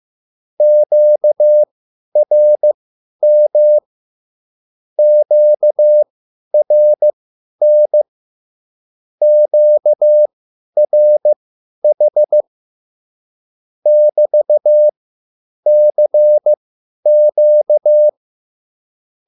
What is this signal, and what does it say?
Morse code, heard as a keyed tone, about 15 wpm, spaced as for 10 wpm, 600 Hz. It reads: QRM QRN QRH <BT>CQ
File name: QSL / CW lesson